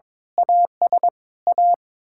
Morse code, heard as a keyed tone, about 21 wpm, 700 Hz. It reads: AS A
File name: practice / 22wpm / Sets of 8 Words - 22wpm